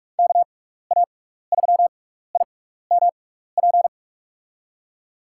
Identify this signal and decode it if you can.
Morse code, heard as a keyed tone, about 42 wpm, spaced as for 15 wpm, 700 Hz. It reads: KA3IMP
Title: USA Call Signs - 6x Character Spacing - 45wpm